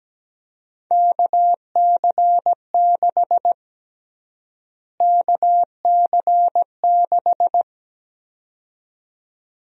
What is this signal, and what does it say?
Morse code, heard as a keyed tone, about 17 wpm, 700 Hz. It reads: KC6 KC6
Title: USA Call Sign Prefixes - Repeated 2x - 17wpm